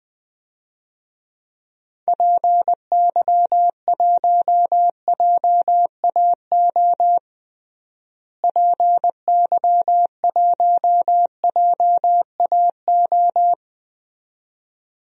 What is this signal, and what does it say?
Morse code, heard as a keyed tone, about 20 wpm, 700 Hz. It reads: PY1JAO PY1JAO
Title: DXCC-100 Call Signs - 6 Characters - Repeated 2x - 20wpm